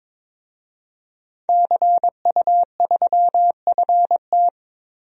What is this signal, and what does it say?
Morse code, heard as a keyed tone, about 22 wpm, 700 Hz. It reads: CU3FT
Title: DXCC-100 Call Signs - 5 Characters - Repeated 3x - 22wpm